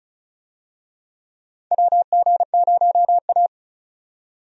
Morse code, heard as a keyed tone, about 34 wpm, 700 Hz. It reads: WG0A
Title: USA Call Signs - 4 Characters - 35wpm